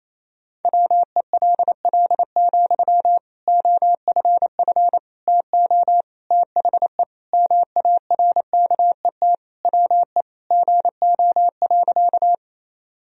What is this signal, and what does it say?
Morse code, heard as a keyed tone, about 28 wpm, 700 Hz. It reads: WELL, OFF TO THE MARKET WE GO.
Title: Sentences from Top 1000 Words - Review Part 3 - 28wpm